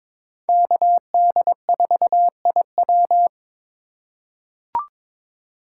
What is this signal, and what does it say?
Morse code, heard as a keyed tone, about 22 wpm, 700 Hz. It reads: KD4IW
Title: USA Call Signs - 5 Characters - 3x Character Spacing - 22wpm